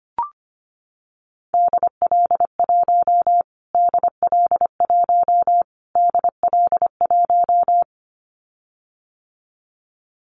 Morse code, heard as a keyed tone, about 25 wpm, 700 Hz. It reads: DL1 DL1 DL1
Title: Callsign Prefixes for Top 25 DXCC - Repeated 3x - 25wpm